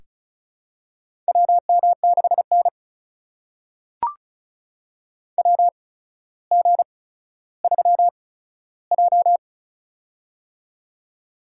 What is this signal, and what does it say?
Morse code, heard as a keyed tone, about 36 wpm, spaced as for 15 wpm, 700 Hz. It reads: WM6N WG3J
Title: USA Call Signs - 8x Character Spacing - 35wpm